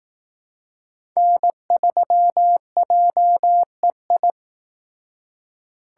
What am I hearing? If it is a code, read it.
Morse code, heard as a keyed tone, about 18 wpm, 700 Hz. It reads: N3JEI
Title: Speed Racing - USA 1x3 Call Signs 15wpm